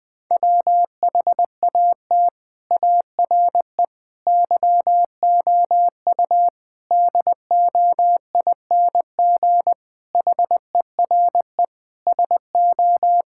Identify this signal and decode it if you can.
Morse code, heard as a keyed tone, about 21 wpm, 700 Hz. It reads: WHAT ARE YOU DOING HERE SO
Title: Sentences with 5-Word Phrases - 20wpm